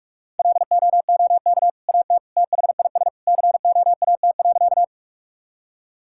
Morse code, heard as a keyed tone, about 41 wpm, 700 Hz. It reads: LOOK AT THIS COAT.
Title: Sentences from Top 1600 Words 45wpm